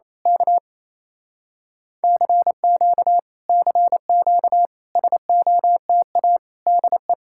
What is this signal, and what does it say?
Morse code, heard as a keyed tone, about 26 wpm, 700 Hz. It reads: K CQ CQ SOTA DE
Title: Speed-Racing Common QSO Phrases Encore - 28wpm